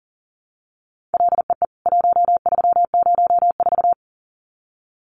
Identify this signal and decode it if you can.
Morse code, heard as a keyed tone, about 40 wpm, 700 Hz. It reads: LEE 1304
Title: CWT Exchange 40wpm